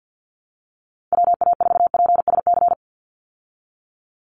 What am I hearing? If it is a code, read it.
Morse code, heard as a keyed tone, about 50 wpm, 700 Hz. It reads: WA4PSC
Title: Call Signs Repeated 2x - 50wpm